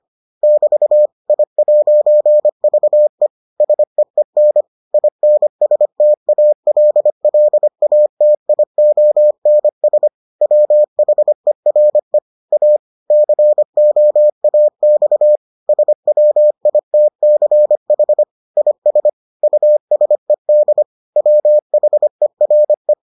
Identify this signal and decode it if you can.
Morse code, heard as a keyed tone, about 25 wpm, 600 Hz. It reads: <BT> I'VE SEEN INSTALLATIONS WHERE A COAX SWITCH IS USED WHERE